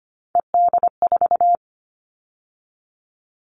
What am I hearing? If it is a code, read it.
Morse code, heard as a keyed tone, about 25 wpm, 700 Hz. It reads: ED4
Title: All Cut Numbers - 3-Digit 25wpm